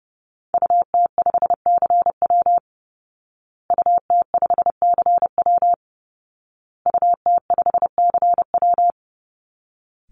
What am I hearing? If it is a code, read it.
Morse code, heard as a keyed tone, about 30 wpm, 700 Hz. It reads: UT5CW UT5CW UT5CW
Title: Call Signs Repeated 2x - 30wpm